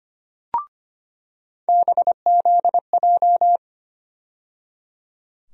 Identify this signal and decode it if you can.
Morse code, heard as a keyed tone, about 25 wpm, 700 Hz. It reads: BZJ
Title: USA Call Sign Postfixes - 3 Letters 25wpm